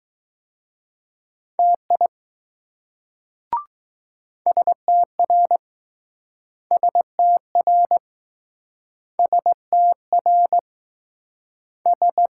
Morse code, heard as a keyed tone, about 20 wpm, 700 Hz. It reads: TI STR STR STR S